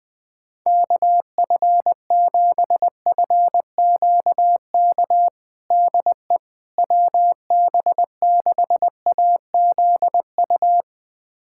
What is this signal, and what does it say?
Morse code, heard as a keyed tone, about 20 wpm, 700 Hz. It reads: KF7FQK DE WB6AZU
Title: USA Callsign DE Callsign - 6 Characters - 20wpm